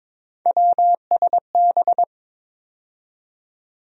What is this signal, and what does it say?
Morse code, heard as a keyed tone, about 22 wpm, 700 Hz. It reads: WSB
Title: USA Call Sign Postfixes - 3 Letters 22wpm